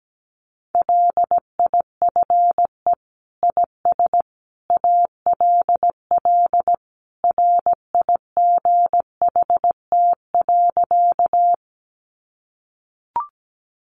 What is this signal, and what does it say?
Morse code, heard as a keyed tone, about 17 wpm, 700 Hz. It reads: LIFE IS ALL RIGHT.
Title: Sentences from Top 300 Words 17wpm